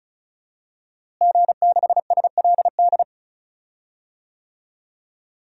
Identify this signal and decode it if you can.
Morse code, heard as a keyed tone, about 36 wpm, 700 Hz. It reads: G6SLD
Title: DXCC-100 Call Signs - 5 Characters - Repeated 3x - 35wpm